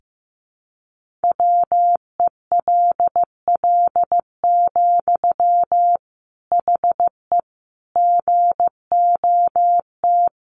Morse code, heard as a keyed tone, about 15 wpm, 700 Hz. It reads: WELL, HE GOT
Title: Sentences from Top 1500 Words 15wpm